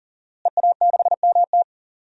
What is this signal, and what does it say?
Morse code, heard as a keyed tone, about 40 wpm, 700 Hz. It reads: EA6MT
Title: DXCC-100 Call Signs - 5 Characters - Repeated 2x - 40wpm